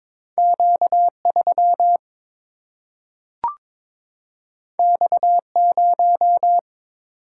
Morse code, heard as a keyed tone, about 22 wpm, 700 Hz. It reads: Q3 X0
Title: Single-Letter Single-Number 22wpm